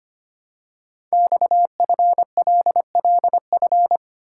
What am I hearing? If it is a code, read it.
Morse code, heard as a keyed tone, about 25 wpm, 700 Hz. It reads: XFLLF
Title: Code Groups - 5 Letters - 25wpm